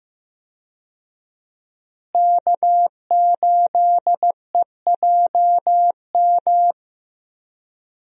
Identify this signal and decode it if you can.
Morse code, heard as a keyed tone, about 15 wpm, 700 Hz. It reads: K8EJM
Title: USA Call Signs - 5 Characters - Repeated 2x - 15wpm